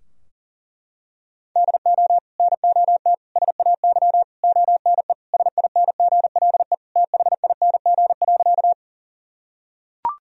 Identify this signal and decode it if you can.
Morse code, heard as a keyed tone, about 40 wpm, 700 Hz. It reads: DO NOT SAY ONE SINGLE THING.
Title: Sentences from Top 1000 Words - Review Part 4 - 40wpm